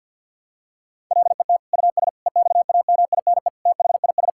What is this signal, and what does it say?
Morse code, heard as a keyed tone, about 50 wpm, 700 Hz. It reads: LET US EXAMINE THIS
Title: Sentences from Top 1600 Words 50wpm